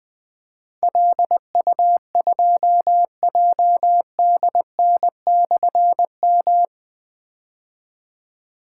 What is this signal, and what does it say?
Morse code, heard as a keyed tone, about 20 wpm, 700 Hz. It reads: LU2JDN/M
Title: Difficult Call Signs 20wpm